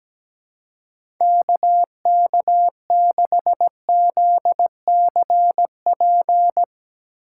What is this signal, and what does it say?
Morse code, heard as a keyed tone, about 17 wpm, 700 Hz. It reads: KK6ZCP
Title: USA Call Signs - 6 Characters - 6x Character Spacing - 17wpm